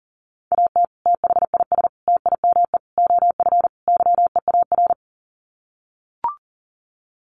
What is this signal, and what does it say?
Morse code, heard as a keyed tone, about 40 wpm, 700 Hz. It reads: AT THIS TIME OF YEAR
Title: Sets of 5 Words Encore2 40wpm